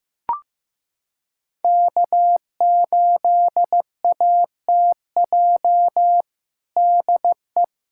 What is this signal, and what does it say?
Morse code, heard as a keyed tone, about 15 wpm, 700 Hz. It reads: K8ATJ DE
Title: USA Callsign DE Callsign - 5 Characters - 15wpm